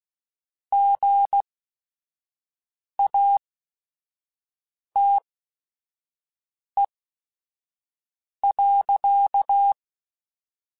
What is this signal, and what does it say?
Morse code, heard as a keyed tone, about 16 wpm, spaced as for 5 wpm, 800 Hz. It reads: GATE.
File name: Technical Subjects / Technical Subjects-mp3s / OR Gate @5